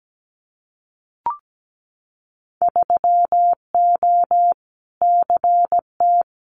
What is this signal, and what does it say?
Morse code, heard as a keyed tone, about 17 wpm, 690 Hz. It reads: 3O CT
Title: Winter Field Day Exchanges 17wpm